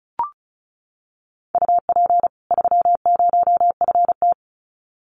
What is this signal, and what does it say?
Morse code, heard as a keyed tone, about 34 wpm, 700 Hz. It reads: UP 30FT
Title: Common QSO Phrases 35wpm